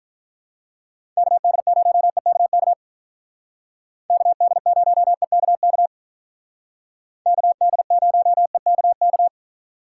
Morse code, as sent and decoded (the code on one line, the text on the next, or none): -.- -.. ----- . -.- -.- / -.- -.. ----- . -.- -.- / -.- -.. ----- . -.- -.-
KD0EKK KD0EKK KD0EKK